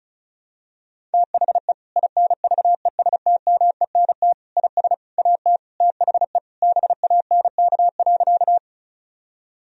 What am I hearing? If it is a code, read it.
Morse code, heard as a keyed tone, about 34 wpm, 700 Hz. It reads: THE INVESTMENT IS AT THE BANK.
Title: Sentences from Top 1000 Words - Review Part 4 - 35wpm